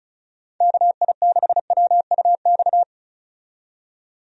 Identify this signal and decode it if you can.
Morse code, heard as a keyed tone, about 36 wpm, 700 Hz. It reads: KI6WUX
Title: USA Call Signs - 6 Characters - 35wpm